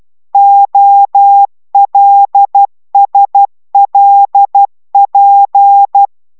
Morse code, heard as a keyed tone, about 12 wpm, 800 Hz. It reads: OLSLP